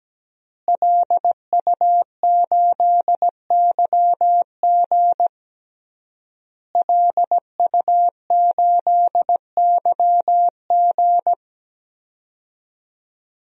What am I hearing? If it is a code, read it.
Morse code, heard as a keyed tone, about 17 wpm, 700 Hz. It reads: LU8YG LU8YG
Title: DXCC-100 Call Signs - 5 Characters - Repeated 3x - 17wpm